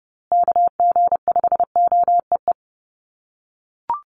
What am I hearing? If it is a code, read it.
Morse code, heard as a keyed tone, about 30 wpm, 700 Hz. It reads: KG5OEE
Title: USA Call Signs Repeated 2x 30wpm